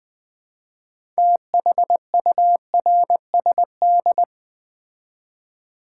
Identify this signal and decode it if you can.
Morse code, heard as a keyed tone, about 20 wpm, 700 Hz. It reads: THURSD
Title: Build A Word - 2 Letters At A Time - 8-Letter Words - 20wpm